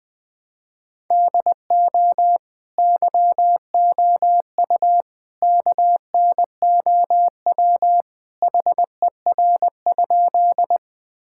Morse code, heard as a keyed tone, about 20 wpm, 700 Hz. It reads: DO YOU KNOW HER?
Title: Sentences with 3-Word Phrases - Spoken Beginning - 20wpm